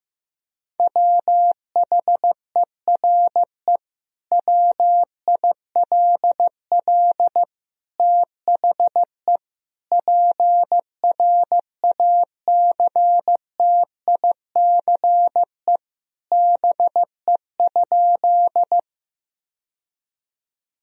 Morse code, heard as a keyed tone, about 15 wpm, 700 Hz. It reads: WHERE WILL THE PRACTICE BE?
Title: Sentences from Top 1000 Words - Review Part 4 - 15wpm